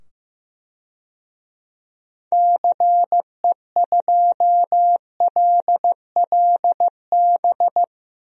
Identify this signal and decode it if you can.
Morse code, heard as a keyed tone, about 16 wpm, 700 Hz. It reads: CE2LLB
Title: DXCC-100 Call Signs - 6 Characters - 15wpm